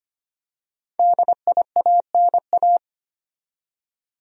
Morse code, heard as a keyed tone, about 25 wpm, 700 Hz. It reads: DIANA
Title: USA Common Names - 25wpm